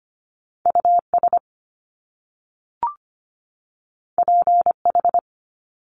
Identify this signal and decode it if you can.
Morse code, heard as a keyed tone, about 25 wpm, 700 Hz. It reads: US PH
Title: Top Two Letter Combinations in Words 25wpm